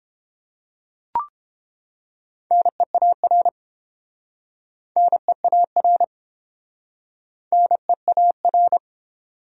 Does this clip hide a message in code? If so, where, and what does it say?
Yes: Morse code from 2.5 to 8.8 s, heard as a keyed tone, about 30 wpm, 700 Hz. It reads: NEAR NEAR NEAR